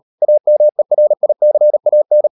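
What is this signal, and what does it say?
Morse code, heard as a keyed tone, about 38 wpm, 600 Hz. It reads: AMERICAN
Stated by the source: Morse code practice at 35–39 wpm sending English text of Ken Follett's novel Never, from CW med Gnister.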